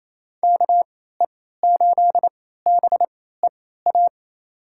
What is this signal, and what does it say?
Morse code, heard as a keyed tone, about 28 wpm, spaced as for 16 wpm, 700 Hz. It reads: KE8BEA
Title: USA Call Signs - 6 Characters - 3x Character Spacing - 28wpm